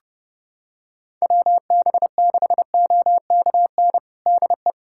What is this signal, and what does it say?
Morse code, heard as a keyed tone, about 30 wpm, 700 Hz. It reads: WB6OKN DE
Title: USA Callsign DE Callsign 30wpm